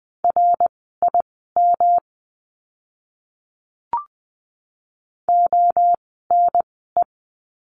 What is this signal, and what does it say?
Morse code, heard as a keyed tone, about 20 wpm, spaced as for 14 wpm, 700 Hz. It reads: RIM ONE
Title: 108 Lesson 18f - Letter m - Words 20wpm